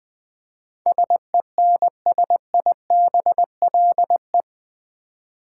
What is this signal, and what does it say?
Morse code, heard as a keyed tone, about 21 wpm, 700 Hz. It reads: SENSIBLE
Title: 8 Letter Words - First 3 Letters Given - 20wpm